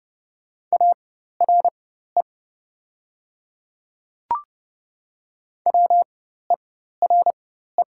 Morse code, heard as a keyed tone, about 30 wpm, spaced as for 14 wpm, 700 Hz. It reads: ARE WERE
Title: 4x Character Spacing - Top 100 Words - 30wpm